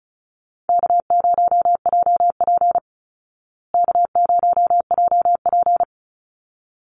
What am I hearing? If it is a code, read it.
Morse code, heard as a keyed tone, about 34 wpm, 700 Hz. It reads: K0JP K0JP
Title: Call Signs Repeated 3x - 35wpm